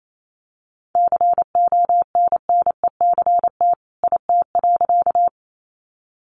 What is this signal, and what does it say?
Morse code, heard as a keyed tone, about 28 wpm, 700 Hz. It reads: CONNECT IT.
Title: Sentences from Top 1500 Words 28wpm